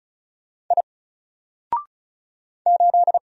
Morse code, heard as a keyed tone, about 34 wpm, 700 Hz. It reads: I 8
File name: Single-Letter-Number-35wpm